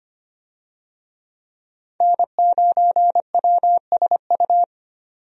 Morse code, heard as a keyed tone, about 25 wpm, 700 Hz. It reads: N9WSU
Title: USA Call Signs - 5 Characters - 25wpm